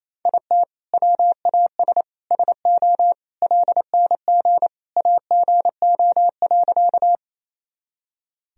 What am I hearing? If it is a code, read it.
Morse code, heard as a keyed tone, about 28 wpm, 700 Hz. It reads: IT WAS SO LNG AGO.
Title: Sentences from Top 500 Words and Abbreviations 28wpm